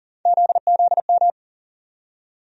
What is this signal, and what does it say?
Morse code, heard as a keyed tone, about 40 wpm, 700 Hz. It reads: ZZM
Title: USA Call Sign Postfixes - 3 Letters 40wpm